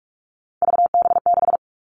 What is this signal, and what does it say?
Morse code, heard as a keyed tone, about 44 wpm, 700 Hz. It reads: VB6
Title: All Cut Numbers - 3-Digit 45wpm